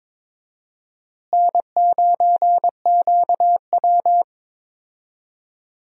Morse code, heard as a keyed tone, about 22 wpm, 700 Hz. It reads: N9QW
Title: USA Call Signs - 4 Characters - Repeated 2x - 22wpm